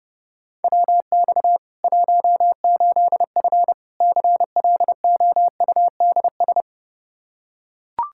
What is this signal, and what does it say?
Morse code, heard as a keyed tone, about 32 wpm, 700 Hz. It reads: WX 18F CLOUDS
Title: Practice QSO - Exchange 2 - Info - 30wpm